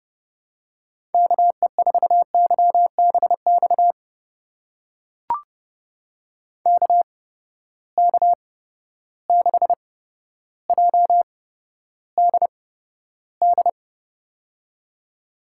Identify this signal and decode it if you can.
Morse code, heard as a keyed tone, about 29 wpm, spaced as for 13 wpm, 700 Hz. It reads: KE4YBX KK6JDD
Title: USA Call Signs - 6 Characters - 8x Character Spacing - 30wpm